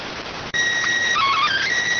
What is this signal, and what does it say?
SSTV (slow-scan television), PD 290.